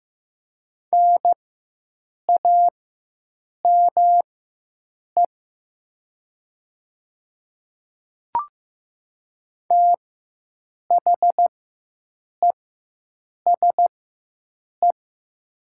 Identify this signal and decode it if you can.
Morse code, heard as a keyed tone, about 15 wpm, spaced as for 7 wpm, 700 Hz. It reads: NAME THESE
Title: 4x Character Spacing - Top 500 Words - 15wpm